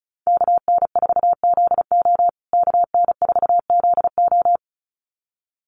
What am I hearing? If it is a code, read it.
Morse code, heard as a keyed tone, about 34 wpm, 700 Hz. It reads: KN4ZO KN4ZO
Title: USA Call Signs Repeated 2x 35wpm